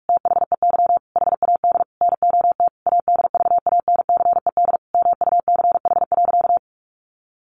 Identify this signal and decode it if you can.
Morse code, heard as a keyed tone, about 44 wpm, 700 Hz. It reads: THEY HAD NOT ADVANCED MUCH.
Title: Sentences from Top 1200 Words 45wpm